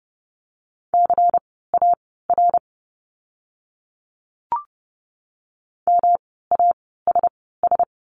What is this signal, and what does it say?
Morse code, heard as a keyed tone, about 30 wpm, spaced as for 17 wpm, 700 Hz. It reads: CAR MASS
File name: course / 30:10wpm / 108 Lesson 18f - Letter m - Words 30wpm